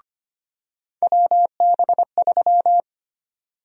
Morse code, heard as a keyed tone, about 25 wpm, 700 Hz. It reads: WB3
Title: USA Call Sign Prefixes - 25wpm